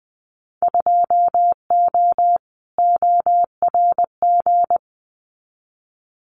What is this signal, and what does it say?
Morse code, heard as a keyed tone, about 20 wpm, 700 Hz. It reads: 2O ORG
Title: Winter Field Day Exchanges 20wpm